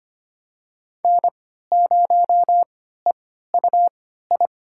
Morse code, heard as a keyed tone, about 25 wpm, spaced as for 14 wpm, 700 Hz. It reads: N0EUI
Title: USA Call Signs - 5 Characters - 3x Character Spacing - 25wpm